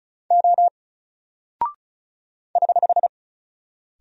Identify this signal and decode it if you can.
Morse code, heard as a keyed tone, about 36 wpm, 700 Hz. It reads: O <HH>
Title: Single Character (Letter:Number:Punctuation:Prosign) Alt 35wpm